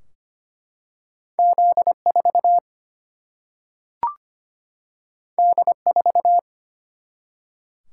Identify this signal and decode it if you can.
Morse code, heard as a keyed tone, about 25 wpm, 700 Hz. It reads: Z4 D4
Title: Single-Letter Single-Number 25wpm